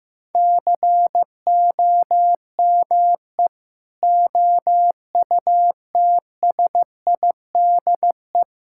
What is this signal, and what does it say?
Morse code, heard as a keyed tone, about 16 wpm, 700 Hz. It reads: COME OUTSIDE
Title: Sentences from Top 1000 Words - Review Part 6 - 15wpm